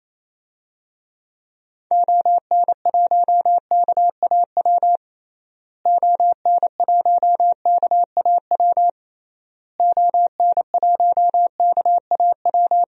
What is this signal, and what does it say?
Morse code, heard as a keyed tone, about 28 wpm, 700 Hz. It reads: ON1KAW ON1KAW ON1KAW
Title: DXCC-100 Call Signs - 6 Characters - Repeated 3x - 28wpm